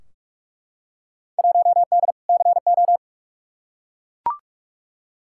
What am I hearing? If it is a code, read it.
Morse code, heard as a keyed tone, about 41 wpm, 700 Hz. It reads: 1D CO
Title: Field Day Exchanges - 45wpm